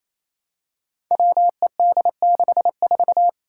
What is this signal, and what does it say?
Morse code, heard as a keyed tone, about 27 wpm, 700 Hz. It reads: WED64
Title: Five Challenging Letter-Number Combinations 28wpm